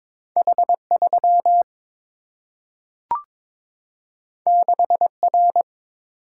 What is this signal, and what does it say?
Morse code, heard as a keyed tone, about 22 wpm, 700 Hz. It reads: H 3 6 R
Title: Single-Letter Single-Number 22wpm